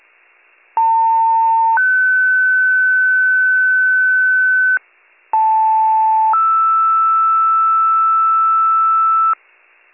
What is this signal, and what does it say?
Two-tone page, heard as a one-second tone, then a three-second tone, about 900 then 1500 Hz, about 880 then 1320 Hz, with noise. It is heard from 0.8 to 9.3 s.